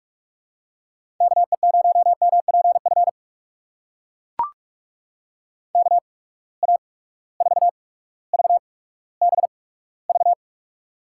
Morse code, heard as a keyed tone, about 43 wpm, spaced as for 19 wpm, 700 Hz. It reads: KE0MPF KA4VBV
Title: USA Call Signs - 6 Characters - 8x Character Spacing - 45wpm